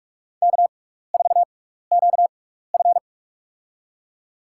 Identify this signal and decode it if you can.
Morse code, heard as a keyed tone, about 46 wpm, spaced as for 16 wpm, 700 Hz. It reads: K4QF
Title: USA Call Signs - 4 Characters - 6x Character Spacing - 45wpm